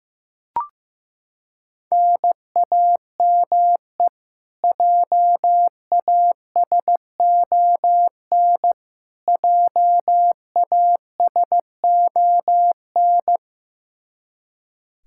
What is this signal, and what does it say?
Morse code, heard as a keyed tone, about 15 wpm, 700 Hz. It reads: NAME JASON JASON
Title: Practice QSO - Exchange 1 - Beginning - 15wpm